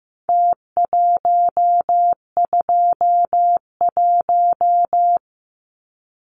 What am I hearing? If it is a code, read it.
Morse code, heard as a keyed tone, about 15 wpm, 700 Hz. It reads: T121
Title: Numbers - Contest Style - 4-Digit 15wpm